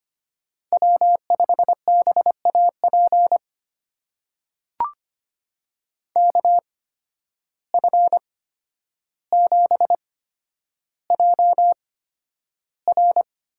Morse code, heard as a keyed tone, about 25 wpm, spaced as for 11 wpm, 700 Hz. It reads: W5BAP KF7JR
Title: USA Call Signs - 5 Characters - 8x Character Spacing - 25wpm